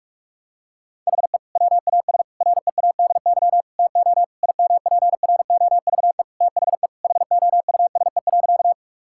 Morse code, heard as a keyed tone, about 46 wpm, 700 Hz. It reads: HE WAS READY TO IMPROVE THE HOUSE.
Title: Sentences from Top 1000 Words - Review Part 5 - 45wpm